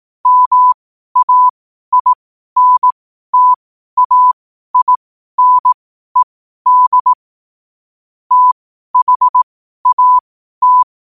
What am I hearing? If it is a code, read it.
Morse code, heard as a keyed tone, about 18 wpm, spaced as for 12 wpm, 1000 Hz. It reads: MAINTAINED THAT
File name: qotdfiles / 12wpm